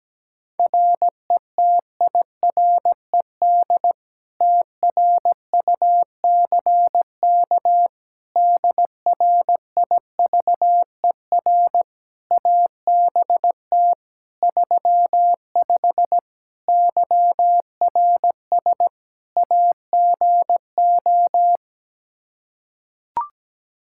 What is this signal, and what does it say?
Morse code, heard as a keyed tone, about 17 wpm, 700 Hz. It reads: RETIRED TRUCK DRIVER ABT 35 YRS AGO
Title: Practice QSO - Exchange 3 - More Info - 17wpm